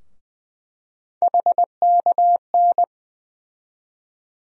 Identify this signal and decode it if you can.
Morse code, heard as a keyed tone, about 20 wpm, 700 Hz. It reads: HKN
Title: USA Call Sign Postfixes - 3 Letters 20wpm